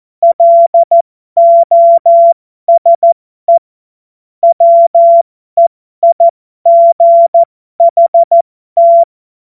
Morse code, heard as a keyed tone, about 14 wpm, spaced as for 12 wpm, 680 Hz. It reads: LOSE WEIGHT
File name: Morsekurs / Witze